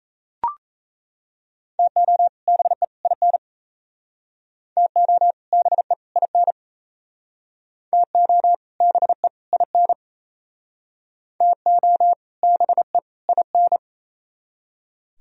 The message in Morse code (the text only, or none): TO BE IN TO BE IN TO BE IN TO BE IN